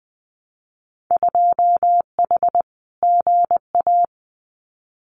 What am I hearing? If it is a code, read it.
Morse code, heard as a keyed tone, about 20 wpm, 700 Hz. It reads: 2H GA
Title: Winter Field Day Exchanges Encore 20wpm